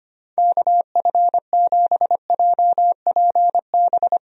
Morse code, heard as a keyed tone, about 25 wpm, 700 Hz. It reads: KF7JPB
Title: USA Call Signs - 6 Characters - 25wpm